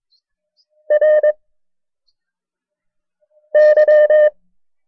Morse code, heard as a keyed tone, about 22 wpm, 610 Hz. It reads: R Y
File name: FreeFiles / CW-practice / KB6MT-tape1-side2